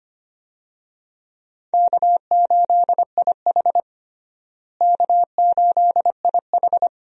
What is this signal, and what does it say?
Morse code, heard as a keyed tone, about 25 wpm, 700 Hz. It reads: K8IH K8IH
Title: USA Call Signs - 4 Characters - Repeated 2x - 25wpm